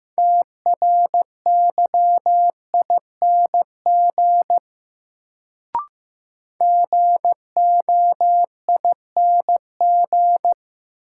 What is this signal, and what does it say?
Morse code, heard as a keyed tone, about 15 wpm, 700 Hz. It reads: TRYING GOING
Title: Sets of 2 Words - New Word Drill - 15wpm